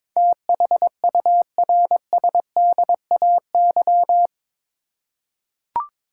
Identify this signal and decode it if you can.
Morse code, heard as a keyed tone, about 22 wpm, 700 Hz. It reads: THURSDAY
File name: practice / 22wpm / Eight Letter Words - 22wpm